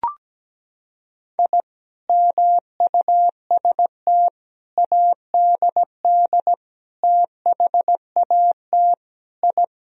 Morse code, heard as a keyed tone, about 17 wpm, 700 Hz. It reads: I MUST ADD THAT I
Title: Sentences from Top 1000 Words - Review Part 1 - 17wpm